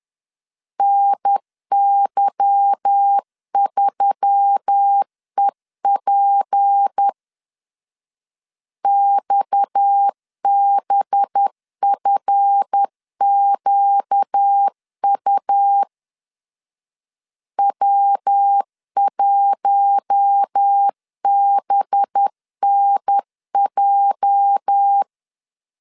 Morse code, heard as a keyed tone, about 10 wpm, 800 Hz. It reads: NY3EP XBFQU W1BNJ
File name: LEZIONI / 8 YJ45 / 8 YJ 10wpm 2651